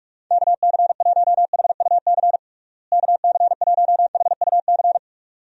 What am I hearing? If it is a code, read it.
Morse code, heard as a keyed tone, about 44 wpm, 700 Hz. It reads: KC1HUC KC1HUC
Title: USA Call Signs - 6 Characters - Repeated 2x - 45wpm